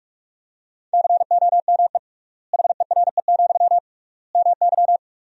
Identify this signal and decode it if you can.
Morse code, heard as a keyed tone, about 46 wpm, 700 Hz. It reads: COME HERE, MY
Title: 3x Word-Spacing - Sentences from Top 300 Words - 45wpm